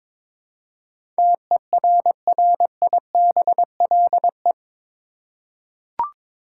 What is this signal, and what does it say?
Morse code, heard as a keyed tone, about 22 wpm, 700 Hz. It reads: TERRIBLE
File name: practice / 22wpm / Eight Letter Words - 22wpm